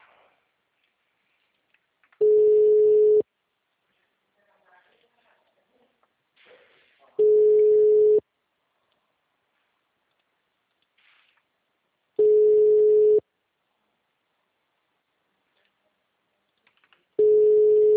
Ringback tone (Europe, 425 Hz).